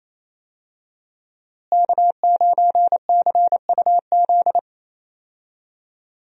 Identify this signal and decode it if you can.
Morse code, heard as a keyed tone, about 28 wpm, 700 Hz. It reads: K9CUZ
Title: USA Call Signs - 5 Characters - Repeated 2x - 28wpm